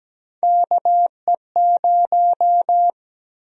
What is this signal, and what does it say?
Morse code, heard as a keyed tone, about 17 wpm, 700 Hz. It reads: KE0
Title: USA Call Sign Prefixes - 17wpm